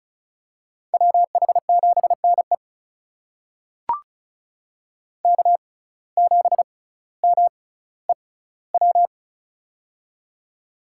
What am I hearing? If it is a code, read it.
Morse code, heard as a keyed tone, about 33 wpm, spaced as for 18 wpm, 700 Hz. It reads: WH7NE K7MEW
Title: USA Call Signs - 5 Characters - 6x Character Spacing - 35wpm